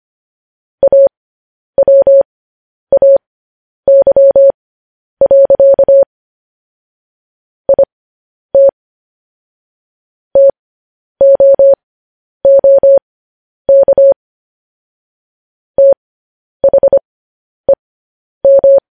Morse code, heard as a keyed tone, about 25 wpm, spaced as for 10 wpm, 550 Hz. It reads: AWAY. IT TOOK THEM